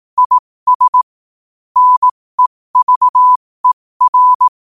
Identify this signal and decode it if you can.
Morse code, heard as a keyed tone, about 18 wpm, spaced as for 15 wpm, 1000 Hz. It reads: IS NEVER
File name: qotdfiles / 15wpm